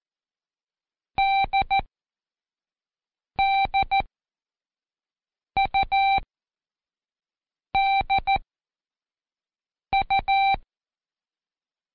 Morse code, heard as a keyed tone, about 14 wpm, spaced as for 5 wpm, 780 Hz. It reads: DDUDU
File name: Code Quick / AudioDownld1 / Seg23-Day3-Intro D